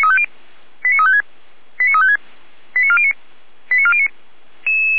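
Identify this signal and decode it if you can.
Five-tone selective call (ZVEI), heard as quick runs of five tones, tones of about 70 ms. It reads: 89356 89356 89409 89409
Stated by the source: ZVEI five-tone selective call 89401, received on an SDR.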